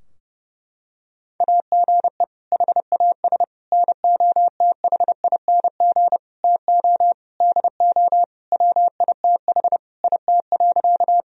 Morse code, heard as a keyed tone, about 30 wpm, 700 Hz. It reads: AGE HAS NOTHING TO DO WITH IT.
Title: Sentences from Top 1000 Words - Review Part 3 - 30wpm